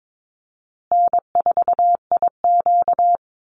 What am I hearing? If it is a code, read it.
Morse code, heard as a keyed tone, about 22 wpm, 690 Hz. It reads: N4IQ
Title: Call Signs Repeated 2x - 22wpm